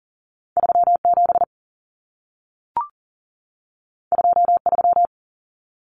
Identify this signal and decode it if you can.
Morse code, heard as a keyed tone, about 40 wpm, 700 Hz. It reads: 37 23
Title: Numbers - 2-Digit 40wpm